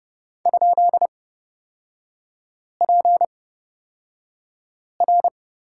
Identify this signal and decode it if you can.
Morse code, heard as a keyed tone, about 29 wpm, spaced as for 5 wpm, 700 Hz. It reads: ?PR
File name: Lesson 36e - BT - Three Characters 30wpm